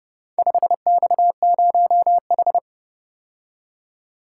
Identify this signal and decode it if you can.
Morse code, heard as a keyed tone, about 31 wpm, 700 Hz. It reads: 5X0H